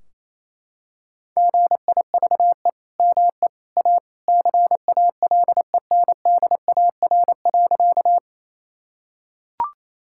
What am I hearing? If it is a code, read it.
Morse code, heard as a keyed tone, about 28 wpm, 700 Hz. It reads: GIVE ME A CALENDAR.
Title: Sentences from Top 1600 Words 28wpm